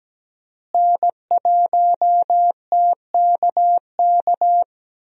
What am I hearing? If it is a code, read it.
Morse code, heard as a keyed tone, about 17 wpm, 700 Hz. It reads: N1TKK
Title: USA Call Signs - 5 Characters - Repeated 3x - 17wpm